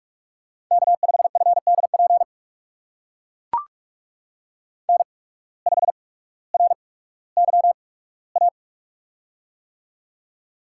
Morse code, heard as a keyed tone, about 44 wpm, spaced as for 19 wpm, 700 Hz. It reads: K5FDP N5RYA
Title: USA Call Signs - 5 Characters - 8x Character Spacing - 45wpm